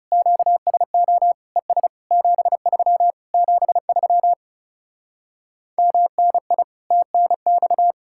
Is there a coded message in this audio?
Yes: Morse code from 0.1 to 7.9 s, heard as a keyed tone, about 40 wpm, 700 Hz. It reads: QSO ES 73 73 MNI TNX